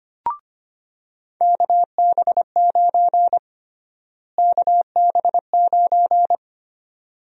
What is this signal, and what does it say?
Morse code, heard as a keyed tone, about 25 wpm, 700 Hz. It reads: KB9 KB9
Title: USA Call Sign Prefixes - Repeated 2x - 25wpm